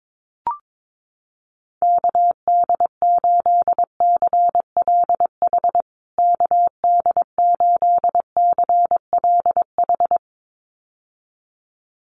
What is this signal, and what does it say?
Morse code, heard as a keyed tone, about 22 wpm, 700 Hz. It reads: KD8CLH KD8CLH
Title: USA Call Signs Repeated 2x 22wpm